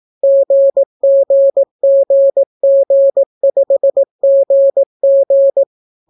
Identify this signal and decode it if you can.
Morse code, heard as a keyed tone, about 18 wpm, 550 Hz. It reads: GGGG5GG